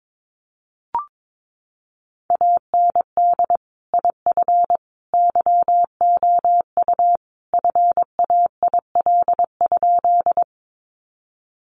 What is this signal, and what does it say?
Morse code, heard as a keyed tone, about 22 wpm, 700 Hz. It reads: AND IF YOU FAIL?
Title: Sentences from Top 1100 Words 22wpm